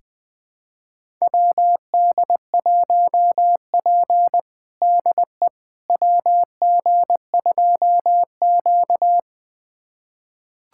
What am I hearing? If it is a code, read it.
Morse code, heard as a keyed tone, about 21 wpm, 700 Hz. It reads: WD1P DE WG2Q
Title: USA Callsign DE Callsign - 4 Characters - 20wpm